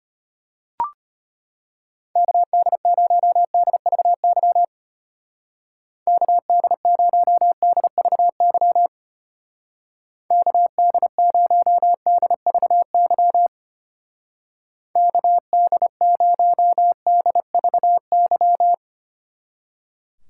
Morse code, heard as a keyed tone, about 32 wpm, 700 Hz. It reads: KD0DVY KD0DVY KD0DVY KD0DVY